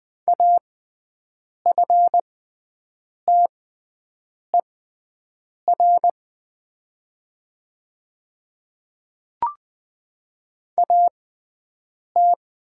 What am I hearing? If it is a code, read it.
Morse code, heard as a keyed tone, about 20 wpm, spaced as for 7 wpm, 690 Hz. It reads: AFTER AT
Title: 6x Character Spacing - Top 200 Words - 20wpm